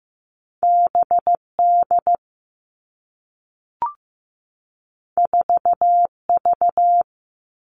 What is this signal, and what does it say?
Morse code, heard as a keyed tone, about 15 wpm, 700 Hz. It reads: BD 4V
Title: All Cut Numbers - 2-Digit 15wpm